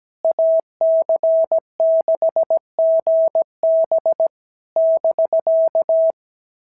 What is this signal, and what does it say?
Morse code, heard as a keyed tone, about 15 wpm, 650 Hz. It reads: AC6GB <BK>
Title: SOTA QSO Practice 17wpm